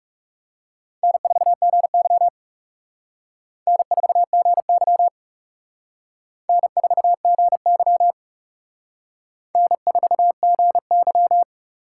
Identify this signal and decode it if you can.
Morse code, heard as a keyed tone, about 37 wpm, 700 Hz. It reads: N4GY N4GY N4GY N4GY
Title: Speed Racing - USA 1x2 Call Signs 30wpm